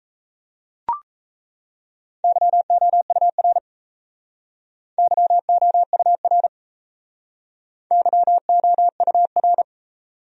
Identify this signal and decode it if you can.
Morse code, heard as a keyed tone, about 36 wpm, 700 Hz. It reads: YOUR YOUR YOUR